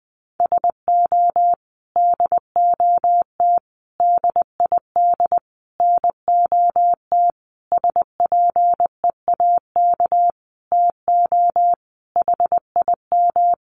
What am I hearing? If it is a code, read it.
Morse code, heard as a keyed tone, about 20 wpm, 700 Hz. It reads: SO DOT DID NOT SPEAK TO HIM
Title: Sentences from Top 1500 Words 20wpm